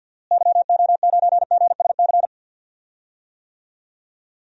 Morse code, heard as a keyed tone, about 52 wpm, 700 Hz. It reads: YO9GSC
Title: DXCC-100 Call Signs - 6 Characters - Repeated 3x - 50wpm